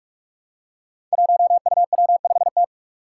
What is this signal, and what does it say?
Morse code, heard as a keyed tone, about 43 wpm, 700 Hz. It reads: 1UW5T
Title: Five Challenging Letter-Number Combinations 45wpm